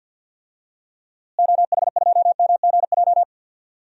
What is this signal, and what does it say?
Morse code, heard as a keyed tone, about 53 wpm, 700 Hz. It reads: OH2MGJ
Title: DXCC-100 Call Signs - 6 Characters - Repeated 3x - 50wpm